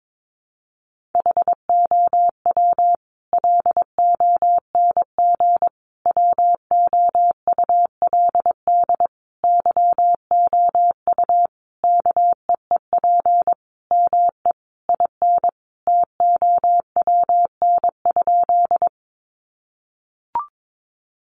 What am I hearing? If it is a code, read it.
Morse code, heard as a keyed tone, about 22 wpm, 700 Hz. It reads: HOW LONG WOULD YOU KEEP ME IN TOWN?
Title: Sentences from Top 500 Words 22wpm